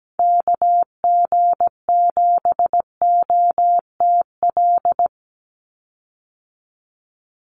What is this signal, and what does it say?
Morse code, heard as a keyed tone, about 17 wpm, 700 Hz. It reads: KG7OTL
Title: USA Call Signs 17wpm